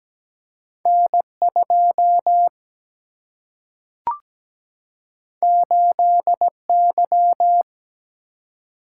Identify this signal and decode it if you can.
Morse code, heard as a keyed tone, about 17 wpm, 700 Hz. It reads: N 2 8 Y
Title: Single-Letter Single-Number 17wpm